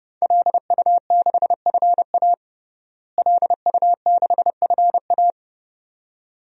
Morse code, heard as a keyed tone, about 30 wpm, 700 Hz. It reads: LU6FA LU6FA
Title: DXCC-100 Call Signs - 5 Characters - Repeated 2x - 30wpm